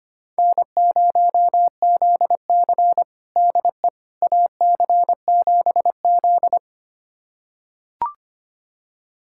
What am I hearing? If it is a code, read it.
Morse code, heard as a keyed tone, about 25 wpm, 700 Hz. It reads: N0ZC DE AC7Z
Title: USA Callsign DE Callsign - 4 Characters - 25wpm